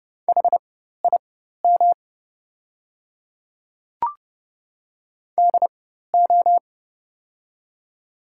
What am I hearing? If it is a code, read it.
Morse code, heard as a keyed tone, about 30 wpm, spaced as for 14 wpm, 700 Hz. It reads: HIM DO
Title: 4x Character Spacing - Top 100 Words - 30wpm